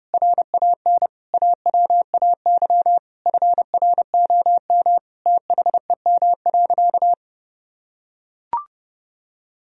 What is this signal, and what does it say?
Morse code, heard as a keyed tone, about 30 wpm, 700 Hz. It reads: RAN AWAY FROM THEM.
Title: Sentences from Top 1000 Words - Review Part 3 - 30wpm